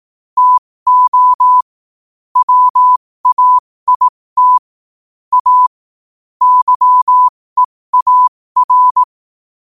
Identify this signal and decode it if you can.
Morse code, heard as a keyed tone, about 18 wpm, spaced as for 15 wpm, 1000 Hz. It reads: TO WAIT A YEAR
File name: qotdfiles / 15wpm